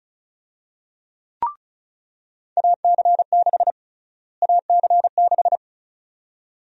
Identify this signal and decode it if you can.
Morse code, heard as a keyed tone, about 36 wpm, 700 Hz. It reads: AC6 AC6
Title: USA Call Sign Prefixes - Repeated 2x - 35wpm